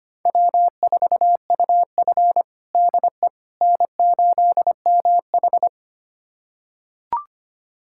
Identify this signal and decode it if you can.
Morse code, heard as a keyed tone, about 25 wpm, 700 Hz. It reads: W4UF DE N8MH
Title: USA Callsign DE Callsign - 4 Characters - 25wpm